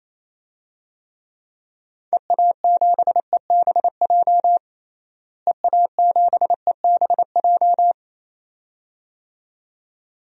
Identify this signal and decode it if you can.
Morse code, heard as a keyed tone, about 28 wpm, 700 Hz. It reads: EA7EBJ EA7EBJ
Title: DXCC-100 Call Signs - 6 Characters - Repeated 2x - 28wpm